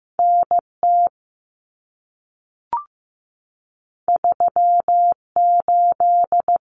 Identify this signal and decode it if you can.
Morse code, heard as a keyed tone, about 15 wpm, 700 Hz. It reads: NT 38
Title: Common Cut Numbers - 2-Digit 15wpm